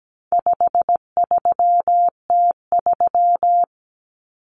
Morse code, heard as a keyed tone, about 17 wpm, 700 Hz. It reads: 53T3
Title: Common Cut Numbers - 4-Digit 17wpm